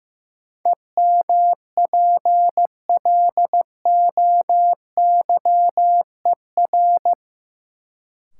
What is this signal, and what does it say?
Morse code, heard as a keyed tone, about 15 wpm, 700 Hz. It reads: EMPLOYER
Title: Sentences from Top 1600 Words - New Word Drill - 15wpm